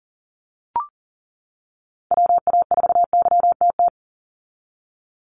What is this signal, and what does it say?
Morse code, heard as a keyed tone, about 39 wpm, 700 Hz. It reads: WA4YTT
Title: USA Call Signs 40wpm